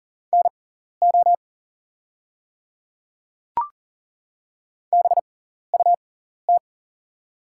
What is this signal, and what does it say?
Morse code, heard as a keyed tone, about 40 wpm, spaced as for 14 wpm, 700 Hz. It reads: NO BUT
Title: 6x Character Spacing - Top 100 Words - 40wpm